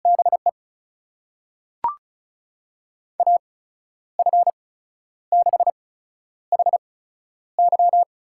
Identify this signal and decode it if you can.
Morse code, heard as a keyed tone, about 36 wpm, spaced as for 11 wpm, 700 Hz. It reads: BE AF6HY
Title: USA Call Signs - 5 Characters - 8x Character Spacing - 35wpm